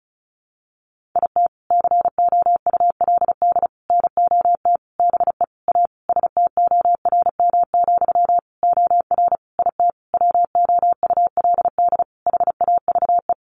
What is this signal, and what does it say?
Morse code, heard as a keyed tone, about 34 wpm, 700 Hz. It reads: IT COULD NOT BE A STORM, OR IT WOULD HAVE
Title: Sentences from Top 1500 Words 35wpm